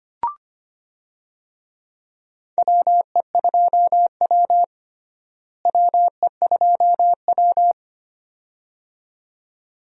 Morse code, heard as a keyed tone, about 25 wpm, 700 Hz. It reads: WE2W WE2W
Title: USA Call Signs - 4 Characters - Repeated 2x - 25wpm